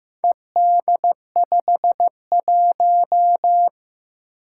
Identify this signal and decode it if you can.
Morse code, heard as a keyed tone, about 15 wpm, 700 Hz. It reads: ED51
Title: Four Challenging Letter-Number Combinations 15wpm